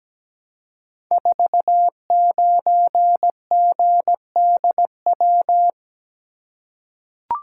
Morse code, heard as a keyed tone, about 17 wpm, 700 Hz. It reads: 49GDW
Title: Code Groups - 5 Letter:Numbers - 17wpm